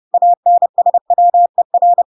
Morse code, heard as a keyed tone, about 30 wpm, 700 Hz. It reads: ANSWER
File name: Pinocchio / Pinocchio ch6 30 30 ik1hge